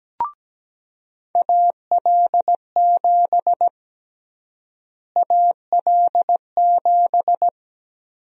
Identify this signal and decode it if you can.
Morse code, heard as a keyed tone, about 17 wpm, 700 Hz. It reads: AL7 AL7
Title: USA Call Sign Prefixes - Repeated 2x - 17wpm